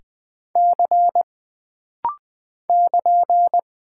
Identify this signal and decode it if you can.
Morse code, heard as a keyed tone, about 20 wpm, 700 Hz. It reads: C <KN>
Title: Single Character (Letter:Number:Punctuation:Prosign) - Rapid-Fire - 20wpm